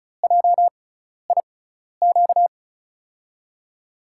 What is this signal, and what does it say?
Morse code, heard as a keyed tone, about 36 wpm, spaced as for 12 wpm, 700 Hz. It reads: JIQ
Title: USA Call Signs - 6 Characters - 6x Character Spacing - 35wpm